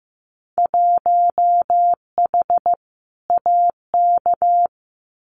Morse code, heard as a keyed tone, about 15 wpm, 700 Hz. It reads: 1H AK
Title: Winter Field Day Exchanges Encore 15wpm